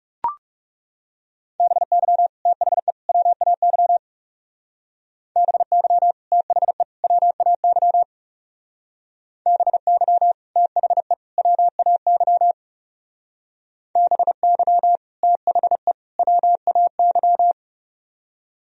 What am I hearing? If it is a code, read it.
Morse code, heard as a keyed tone, about 36 wpm, 700 Hz. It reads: BY THE WAY BY THE WAY BY THE WAY BY THE WAY
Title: Speed-Racing - Sets of 3 Words 30wpm